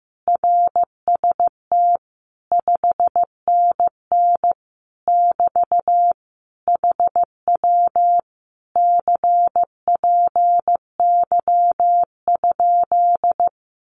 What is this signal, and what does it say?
Morse code, heard as a keyed tone, about 15 wpm, 700 Hz. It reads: RST 5NN <BT> HW CPY?
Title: Common QSO Phrases Encore 15wpm